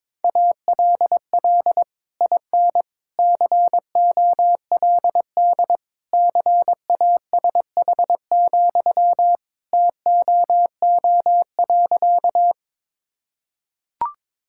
Morse code, heard as a keyed tone, about 21 wpm, 700 Hz. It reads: ALL IN COLD CASH, TOO.
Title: Sentences from Top 1000 Words - Review Part 5 - 22wpm